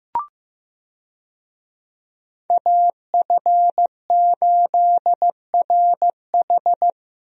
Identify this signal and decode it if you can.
Morse code, heard as a keyed tone, about 16 wpm, 700 Hz. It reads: AF8RH
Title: USA Call Signs - 5 Characters - Repeated 2x - 15wpm